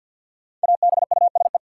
Morse code, heard as a keyed tone, about 53 wpm, 700 Hz. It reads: ABUSE
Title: Sentences from Top 1300 Words - New Word Drill Alt - 50wpm